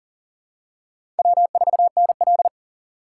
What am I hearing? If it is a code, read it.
Morse code, heard as a keyed tone, about 40 wpm, 700 Hz. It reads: W4NL
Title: USA Call Signs - 4 Characters - 4x Character Spacing - 40wpm